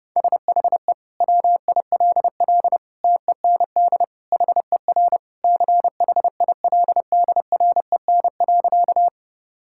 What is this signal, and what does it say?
Morse code, heard as a keyed tone, about 30 wpm, 700 Hz. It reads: SHE WILL TEND HER CHILDREN.
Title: Sentences from Top 1000 Words - Review Part 5 - 30wpm